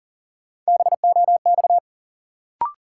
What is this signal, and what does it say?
Morse code, heard as a keyed tone, about 37 wpm, 700 Hz. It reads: BOX
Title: Sentences from Top 500 Words - New Word Drill - 40wpm